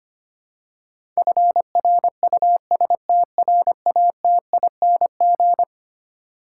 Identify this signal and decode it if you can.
Morse code, heard as a keyed tone, about 25 wpm, 700 Hz. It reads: FRUSTRATING
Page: morse-code-ninja-practice-25wpm